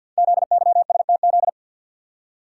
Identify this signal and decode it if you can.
Morse code, heard as a keyed tone, about 48 wpm, 700 Hz. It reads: ZYSTZ
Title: Code Groups - 5 Letters - 50wpm